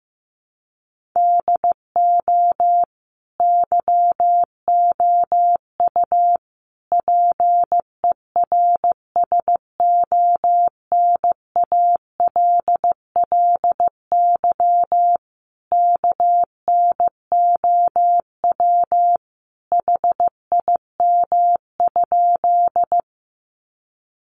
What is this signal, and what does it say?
Morse code, heard as a keyed tone, about 15 wpm, 700 Hz. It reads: DO YOU PERSONALLY KNOW HIM?
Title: Sentences from Top 1400 Words 15wpm